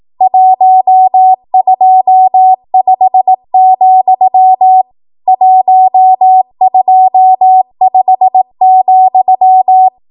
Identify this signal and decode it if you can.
Morse code, heard as a keyed tone, about 18 wpm, 750 Hz. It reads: 125, 125,
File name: Morse / 18WPM